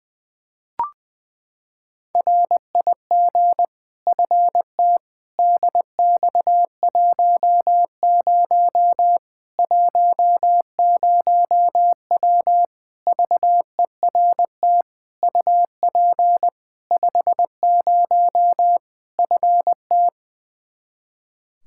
Morse code, heard as a keyed tone, about 20 wpm, 700 Hz. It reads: RIG FT DX10 10W VERT UP 50 FT
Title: Practice QSO - Exchange 2 - Info - 20wpm